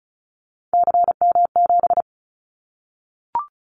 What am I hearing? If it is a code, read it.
Morse code, heard as a keyed tone, about 36 wpm, 700 Hz. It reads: CM7
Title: Callsign Prefixes for Top 100 DXCC - 35wpm